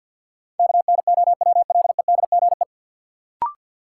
Morse code, heard as a keyed tone, about 50 wpm, 700 Hz. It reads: KNOWLEDGE
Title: Sentences from Top 600 Words - New Word Drill - 50wpm